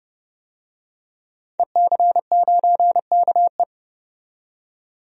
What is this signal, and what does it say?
Morse code, heard as a keyed tone, about 32 wpm, 700 Hz. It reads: EC9KE
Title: DXCC-100 Call Signs - 5 Characters - 30wpm